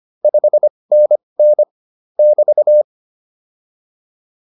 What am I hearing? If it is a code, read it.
Morse code, heard as a keyed tone, about 25 wpm, spaced as for 20 wpm, 600 Hz. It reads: 5NN <BT>
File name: QSO201a 20